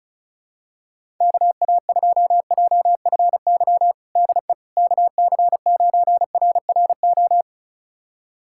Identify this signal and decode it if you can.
Morse code, heard as a keyed tone, about 36 wpm, 700 Hz. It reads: KA2JFY DE KC9RRO
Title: USA Callsign DE Callsign - 6 Characters - 35wpm